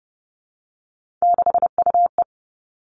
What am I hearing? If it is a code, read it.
Morse code, heard as a keyed tone, about 30 wpm, 700 Hz. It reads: 6UE
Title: Three Challenging Letter-Number Combinations 30wpm